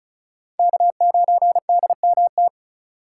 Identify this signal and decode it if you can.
Morse code, heard as a keyed tone, about 34 wpm, 700 Hz. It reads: K9DMT
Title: USA Call Signs - 5 Characters - 4x Character Spacing - 35wpm